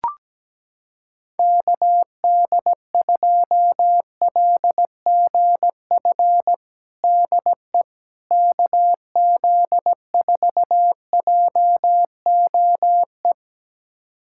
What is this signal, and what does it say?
Morse code, heard as a keyed tone, about 17 wpm, 700 Hz. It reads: KD2LGF DE KZ4JOE
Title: USA Callsign DE Callsign - 6 Characters - 17wpm